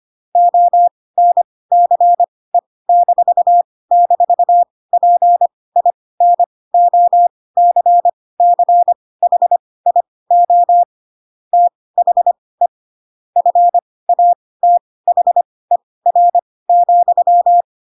Morse code, heard as a keyed tone, about 25 wpm, spaced as for 18 wpm, 700 Hz. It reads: ONCE--PINOCCHIO THE FATHER,
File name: Pinocchio ch3 12 25 ik1hge